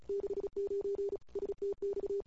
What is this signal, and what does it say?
Morse code, heard as a keyed tone, about 37 wpm, 400 Hz. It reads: 69 STX
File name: CW / 2008-ss-cw-k5tr-op-n5rz -22:00